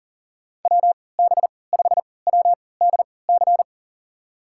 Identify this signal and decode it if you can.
Morse code, heard as a keyed tone, about 40 wpm, spaced as for 23 wpm, 700 Hz. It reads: WB5WDC
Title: USA Call Signs - 6 Characters - 3x Character Spacing - 40wpm